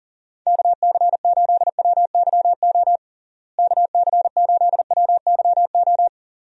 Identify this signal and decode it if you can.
Morse code, heard as a keyed tone, about 40 wpm, 700 Hz. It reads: KC8WYO KC8WYO
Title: USA Call Signs - 6 Characters - Repeated 2x - 40wpm